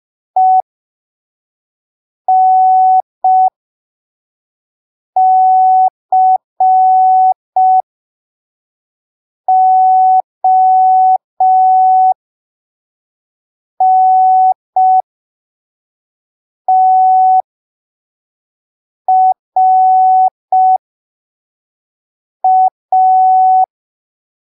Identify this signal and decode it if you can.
Morse code, heard as a keyed tone, about 5 wpm, spaced as for 3 wpm, 750 Hz. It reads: ENCONTRA
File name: arquivos / texto / texto 23 05ppm